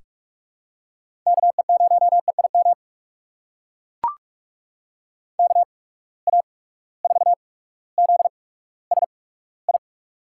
Morse code, heard as a keyed tone, about 42 wpm, spaced as for 19 wpm, 700 Hz. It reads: KE0EIM KA4ZSI
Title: USA Call Signs - 6 Characters - 8x Character Spacing - 45wpm